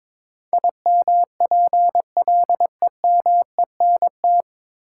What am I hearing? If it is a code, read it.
Morse code, heard as a keyed tone, about 22 wpm, 700 Hz. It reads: IMPLEMENT
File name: 9 Letter Words - First 3 Letters Given - 22wpm